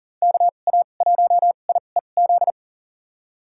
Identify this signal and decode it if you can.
Morse code, heard as a keyed tone, about 40 wpm, spaced as for 29 wpm, 700 Hz. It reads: KA1IEZ
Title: USA Call Signs - 2x Character Spacing - 40wpm